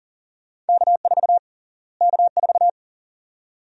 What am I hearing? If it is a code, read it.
Morse code, heard as a keyed tone, about 38 wpm, 700 Hz. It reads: K4 K4
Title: USA Call Sign Prefixes - Repeated 2x - 40wpm